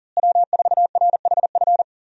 Morse code, heard as a keyed tone, about 40 wpm, 700 Hz. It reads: W4RHF
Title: USA Call Signs - 5 Characters - Repeated 3x - 40wpm